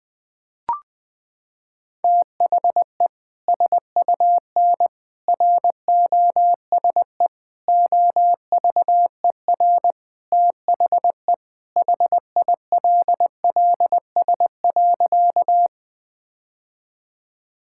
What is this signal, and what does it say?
Morse code, heard as a keyed tone, about 20 wpm, 700 Hz. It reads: THE SUN ROSE OVER THE HILLS.
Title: Sentences with 2-Word Phrases - 20wpm